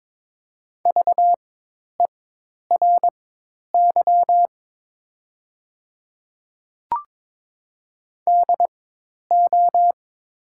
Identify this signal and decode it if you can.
Morse code, heard as a keyed tone, about 21 wpm, spaced as for 10 wpm, 700 Hz. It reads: VERY DO
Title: 4x Character Spacing - Top 200 Words - 22wpm